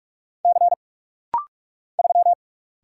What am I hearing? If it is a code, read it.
Morse code, heard as a keyed tone, about 44 wpm, 700 Hz. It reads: C 3
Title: Single Letter-Number Alt - Rapid-Fire 45wpm